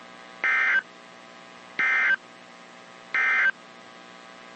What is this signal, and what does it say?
Emergency Alert System (SAME) tones, heard as harsh buzzing bursts. It is heard from 0.3 to 3.6 s.
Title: WZ2523 - NOAA Weather Radio